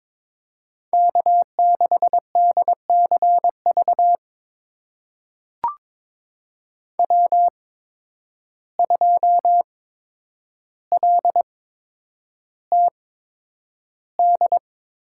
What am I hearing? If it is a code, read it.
Morse code, heard as a keyed tone, about 22 wpm, spaced as for 9 wpm, 700 Hz. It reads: K6DCV W2LTD
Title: USA Call Signs - 5 Characters - 8x Character Spacing - 22wpm